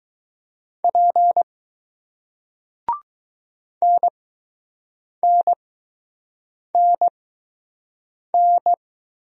Morse code, heard as a keyed tone, about 20 wpm, spaced as for 6 wpm, 700 Hz. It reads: P NNNN